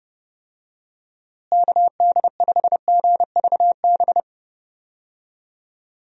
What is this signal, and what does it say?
Morse code, heard as a keyed tone, about 30 wpm, 700 Hz. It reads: KD5GVB
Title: USA Call Signs - 6 Characters - Repeated 2x - 30wpm